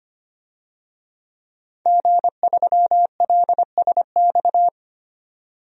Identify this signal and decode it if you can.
Morse code, heard as a keyed tone, about 25 wpm, 700 Hz. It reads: G3LSX
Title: DXCC-100 Call Signs - 5 Characters - 25wpm